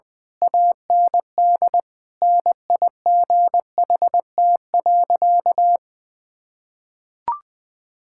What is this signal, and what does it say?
Morse code, heard as a keyed tone, about 20 wpm, 700 Hz. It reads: AND NIGHT.
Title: Sentences from Top 1000 Words - Review Part 6 - 20wpm